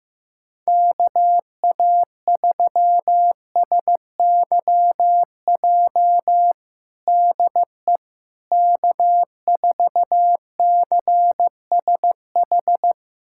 Morse code, heard as a keyed tone, about 16 wpm, 700 Hz. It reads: KA3SYJ DE K4CSH
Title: USA Callsign DE Callsign 15wpm